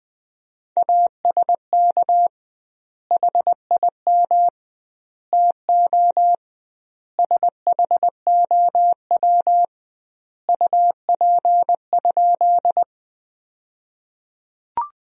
Morse code, heard as a keyed tone, about 21 wpm, 700 Hz. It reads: ASK HIM TO SHOW UP?
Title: 2x Word-Spacing - Sentences from Top 200 Words - 20wpm